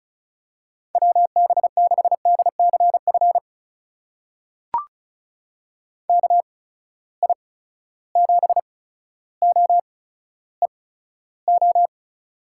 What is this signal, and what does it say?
Morse code, heard as a keyed tone, about 36 wpm, spaced as for 15 wpm, 700 Hz. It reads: WB6DCF KI7OEO
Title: USA Call Signs - 6 Characters - 8x Character Spacing - 35wpm